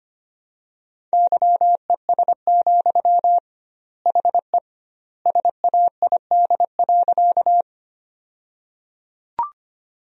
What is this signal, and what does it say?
Morse code, heard as a keyed tone, about 25 wpm, 700 Hz. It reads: YES, HE SAID.
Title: 2x Word-Spacing - Sentences from Top 500 Words - 25wpm